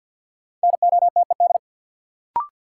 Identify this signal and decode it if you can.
Morse code, heard as a keyed tone, about 50 wpm, 700 Hz. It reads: NOTED
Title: Sets of 5 Words - New Word Drill - 50wpm